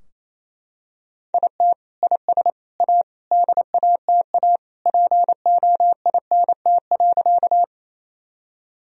Morse code, heard as a keyed tone, about 28 wpm, 700 Hz. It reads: IT IS A DATA POINT.
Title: Sentences from Top 1000 Words - Review Part 3 - 28wpm